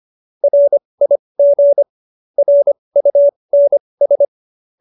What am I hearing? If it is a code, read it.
Morse code, heard as a keyed tone, about 25 wpm, spaced as for 20 wpm, 570 Hz. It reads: RIG RUNS
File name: QSO317 20